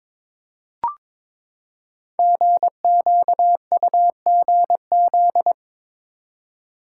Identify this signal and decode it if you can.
Morse code, heard as a keyed tone, about 22 wpm, 700 Hz. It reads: GQUGZ
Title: Code Groups - 5 Letters - 22wpm